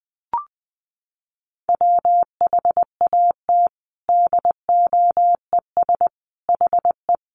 Morse code, heard as a keyed tone, about 20 wpm, 700 Hz. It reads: WHAT DOES HE
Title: Sentences from Top 1300 Words 20wpm